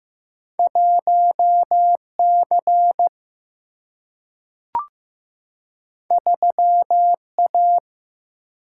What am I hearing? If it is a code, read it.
Morse code, heard as a keyed tone, about 15 wpm, 700 Hz. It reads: 1 C 3 A
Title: Single-Letter Single-Number 15wpm